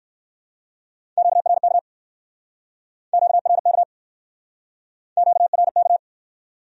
Morse code, heard as a keyed tone, about 60 wpm, 700 Hz. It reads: QRK QRK QRK